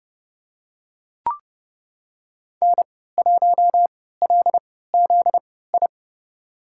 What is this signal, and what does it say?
Morse code, heard as a keyed tone, about 30 wpm, spaced as for 17 wpm, 700 Hz. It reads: N1LZI
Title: USA Call Signs - 3x Character Spacing - 30wpm